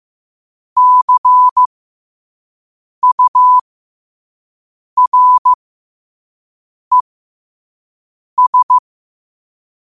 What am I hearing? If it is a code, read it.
Morse code, heard as a keyed tone, about 15 wpm, spaced as for 5 wpm, 1000 Hz. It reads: CURES
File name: qotdfiles / 5wpm